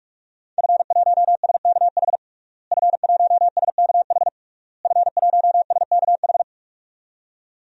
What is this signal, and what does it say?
Morse code, heard as a keyed tone, about 46 wpm, 700 Hz. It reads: F1SKH F1SKH F1SKH
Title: DXCC-100 Call Signs - 5 Characters - Repeated 3x - 45wpm